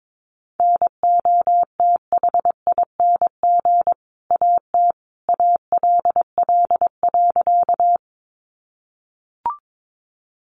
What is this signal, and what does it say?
Morse code, heard as a keyed tone, about 22 wpm, 700 Hz. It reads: NOTHING AT ALL.
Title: Sentences from Top 500 Words 22wpm